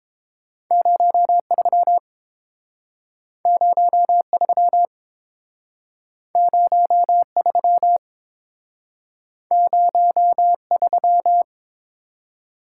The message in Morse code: 03 03 03 03

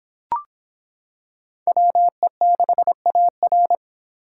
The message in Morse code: WE6AR

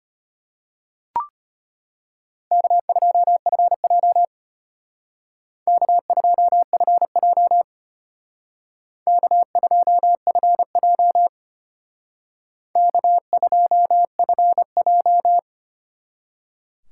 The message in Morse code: K2FJ K2FJ K2FJ K2FJ

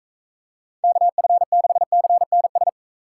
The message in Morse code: KF6CNS

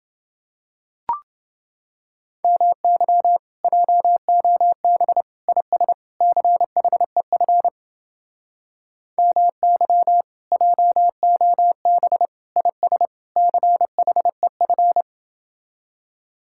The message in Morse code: MY JOB IS CHEF MY JOB IS CHEF

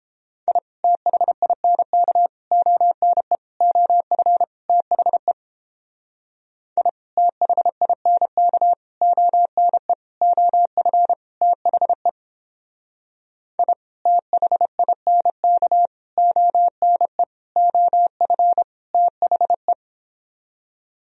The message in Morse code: I THINK ONE OF THE I THINK ONE OF THE I THINK ONE OF THE

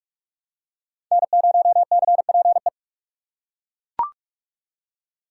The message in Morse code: N0CPE

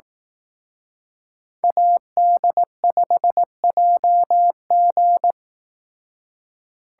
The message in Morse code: AD5JG

